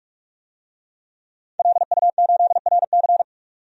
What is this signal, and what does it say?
Morse code, heard as a keyed tone, about 48 wpm, 700 Hz. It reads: LU8RC
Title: DXCC-100 Call Signs - 5 Characters - 45wpm